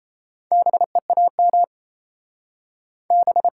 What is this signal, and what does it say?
Morse code, heard as a keyed tone, about 37 wpm, 700 Hz. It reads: BEAM B